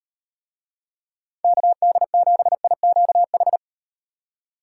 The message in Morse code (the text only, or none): KD7IQH